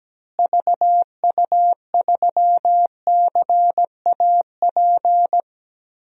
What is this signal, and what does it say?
Morse code, heard as a keyed tone, about 17 wpm, 700 Hz. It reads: VU3CAP
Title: DXCC-100 Call Signs - 6 Characters - Repeated 3x - 17wpm